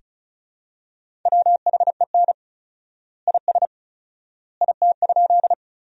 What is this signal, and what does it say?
Morse code, heard as a keyed tone, about 36 wpm, 700 Hz. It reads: WHEN IS IT?
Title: 4x Word-Spacing - Sentences from Top 100 Words - 35wpm